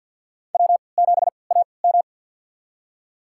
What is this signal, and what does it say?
Morse code, heard as a keyed tone, about 53 wpm, spaced as for 29 wpm, 700 Hz. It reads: W7AM
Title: USA Call Signs - 4 Characters - 3x Character Spacing - 50wpm